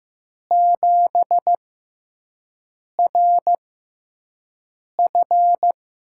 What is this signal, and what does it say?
Morse code, heard as a keyed tone, about 15 wpm, spaced as for 5 wpm, 700 Hz. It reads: 7RF
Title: USA Call Signs - 6x Character Spacing - 15wpm